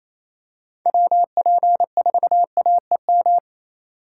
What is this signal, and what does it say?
Morse code, heard as a keyed tone, about 28 wpm, 700 Hz. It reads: WP4AEM
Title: USA Call Signs - 2x Character Spacing - 28wpm